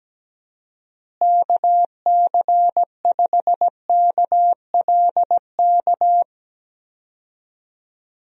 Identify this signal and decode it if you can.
Morse code, heard as a keyed tone, about 17 wpm, 700 Hz. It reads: KC5KLK
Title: USA Call Signs - 6 Characters - Repeated 2x - 17wpm